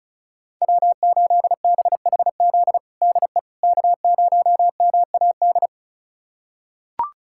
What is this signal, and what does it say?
Morse code, heard as a keyed tone, about 36 wpm, 690 Hz. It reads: W8BHZ DE K0MAD
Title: USA Callsign DE Callsign - 5 Characters - 35wpm